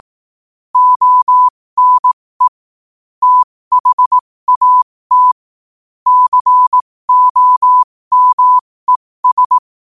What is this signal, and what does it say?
Morse code, heard as a keyed tone, about 18 wpm, spaced as for 15 wpm, 1000 Hz. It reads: ONE THAT COMES
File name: 15wpm